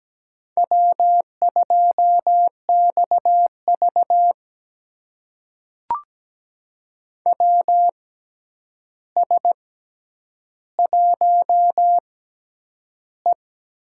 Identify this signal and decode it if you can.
Morse code, heard as a keyed tone, about 17 wpm, spaced as for 9 wpm, 700 Hz. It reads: W2XV WS1E
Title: USA Call Signs - 4 Characters - 6x Character Spacing - 17wpm